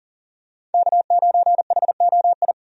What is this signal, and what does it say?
Morse code, heard as a keyed tone, about 40 wpm, 700 Hz. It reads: K9HOI
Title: USA Call Signs - 5 Characters - 40wpm